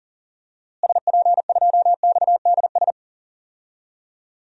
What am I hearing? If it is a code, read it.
Morse code, heard as a keyed tone, about 40 wpm, 700 Hz. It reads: SP2XDS
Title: DXCC-100 Call Signs - 6 Characters - Repeated 2x - 40wpm